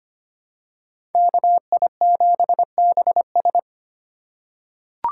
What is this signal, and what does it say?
Morse code, heard as a keyed tone, about 25 wpm, 700 Hz. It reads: KI7BS
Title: USA Call Signs - 5 Characters - 4x Character Spacing - 25wpm